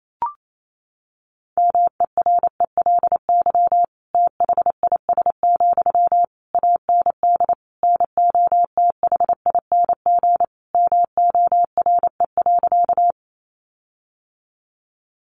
Morse code, heard as a keyed tone, about 28 wpm, 700 Hz. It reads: MERELY THIS, AND NOTHING MORE.
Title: Sentences from Top 1400 Words 28wpm